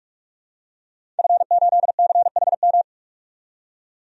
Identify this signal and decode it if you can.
Morse code, heard as a keyed tone, about 46 wpm, 700 Hz. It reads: F8CHM
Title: DXCC-100 Call Signs - 5 Characters - 45wpm